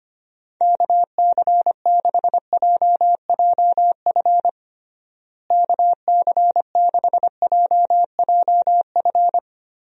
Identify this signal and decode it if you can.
Morse code, heard as a keyed tone, about 25 wpm, 700 Hz. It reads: KC6JJF KC6JJF
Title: USA Call Signs - 6 Characters - Repeated 2x - 25wpm